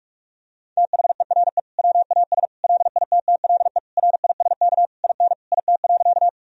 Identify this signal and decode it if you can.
Morse code, heard as a keyed tone, about 46 wpm, 700 Hz. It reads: THERE WAS LITTLE RISK IN IT.
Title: Sentences from Top 1000 Words - Review Part 4 - 45wpm